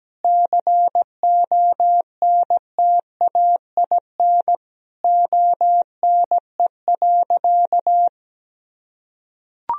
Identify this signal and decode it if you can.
Morse code, heard as a keyed tone, about 17 wpm, 700 Hz. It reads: CONTAIN ONE.
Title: Sentences from Top 1000 Words - Review Part 2 - 17wpm